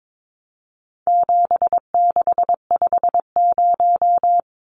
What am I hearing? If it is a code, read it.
Morse code, heard as a keyed tone, about 22 wpm, 700 Hz. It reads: 7650
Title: Numbers - 4-Digit 22wpm